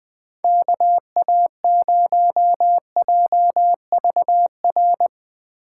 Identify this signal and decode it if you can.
Morse code, heard as a keyed tone, about 19 wpm, 700 Hz. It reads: KA0JVR
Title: Speed Racing - USA 2x3 Call Signs 17wpm